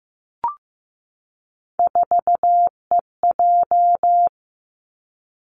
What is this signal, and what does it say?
Morse code, heard as a keyed tone, about 15 wpm, 700 Hz. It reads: 4EJ
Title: Three Challenging Letter-Number Combinations 15wpm